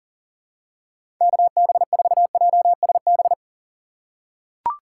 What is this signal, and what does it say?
Morse code, heard as a keyed tone, about 40 wpm, 700 Hz. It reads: KB4JSB
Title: USA Call Signs - 6 Characters - 4x Character Spacing - 40wpm